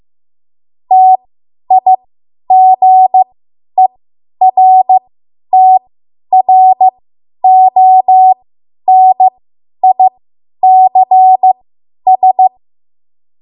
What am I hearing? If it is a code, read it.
Morse code, heard as a keyed tone, about 15 wpm, spaced as for 10 wpm, 750 Hz. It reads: TIGERTRONICS
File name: Morse / 10WPM